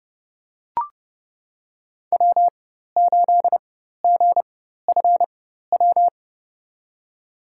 Morse code, heard as a keyed tone, about 31 wpm, spaced as for 14 wpm, 700 Hz. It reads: W8GFW
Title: USA Call Signs - 5 Characters - 4x Character Spacing - 30wpm